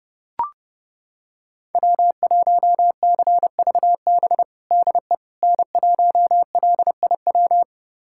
Morse code, heard as a keyed tone, about 30 wpm, 700 Hz. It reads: W1CVB DE N1LIW
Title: USA Callsign DE Callsign - 5 Characters - 30wpm